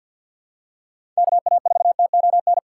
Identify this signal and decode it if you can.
Morse code, heard as a keyed tone, about 49 wpm, 690 Hz. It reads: KA4TON
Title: USA Call Signs - 6 Characters - 3x Character Spacing - 50wpm